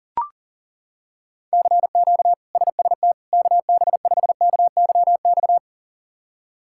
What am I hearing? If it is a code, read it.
Morse code, heard as a keyed tone, about 37 wpm, 700 Hz. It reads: CQ SST KB5KYX
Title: K1USN SST QSO Practice - 40wpm